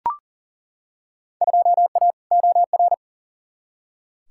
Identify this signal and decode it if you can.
Morse code, heard as a keyed tone, about 40 wpm, 700 Hz. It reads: 2A OR
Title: Field Day Exchanges - 40wpm